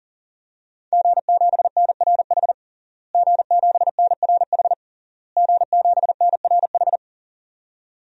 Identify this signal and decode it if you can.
Morse code, heard as a keyed tone, about 37 wpm, 700 Hz. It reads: G7NRH G7NRH G7NRH
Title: DXCC-100 Call Signs - 5 Characters - Repeated 3x - 40wpm